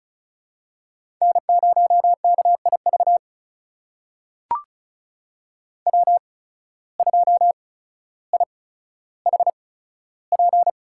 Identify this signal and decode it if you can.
Morse code, heard as a keyed tone, about 36 wpm, spaced as for 15 wpm, 700 Hz. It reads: N0KIV W2IHP
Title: USA Call Signs - 5 Characters - 8x Character Spacing - 35wpm